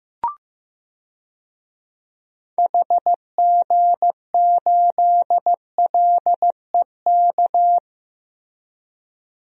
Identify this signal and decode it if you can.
Morse code, heard as a keyed tone, about 15 wpm, 700 Hz. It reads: HG8LEK
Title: DXCC-100 Call Signs - 6 Characters - Repeated 2x - 15wpm